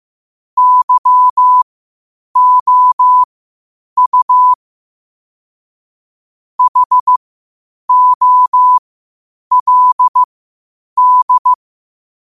Morse code, heard as a keyed tone, about 15 wpm, spaced as for 8 wpm, 1000 Hz. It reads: YOU HOLD
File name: qotdfiles / 8wpm